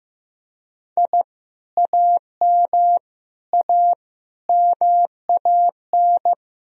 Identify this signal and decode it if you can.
Morse code, heard as a keyed tone, about 15 wpm, 700 Hz. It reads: I AM A MAN
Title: Sentences from Top 1000 Words - Review Part 2 - 15wpm